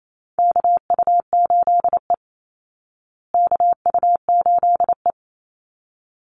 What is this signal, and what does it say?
Morse code, heard as a keyed tone, about 28 wpm, 700 Hz. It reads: KU8E KU8E
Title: Call Signs Repeated 3x - 28wpm